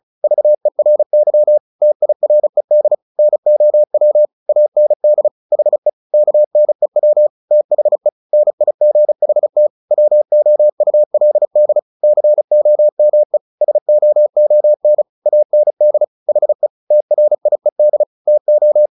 Morse code, heard as a keyed tone, about 34 wpm, 600 Hz. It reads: VERY TIRED NOW AND HE KNEW THE NIGHT WOULD COME SOON AND HE TRIED TO
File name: the old man and the sea / the old man and the sea 1 35wpm